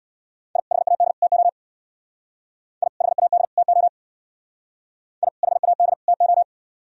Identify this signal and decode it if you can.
Morse code, heard as a keyed tone, about 61 wpm, 700 Hz. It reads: I HAD TO I HAD TO I HAD TO
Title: Speed-Racing - Sets of 3 Words 50wpm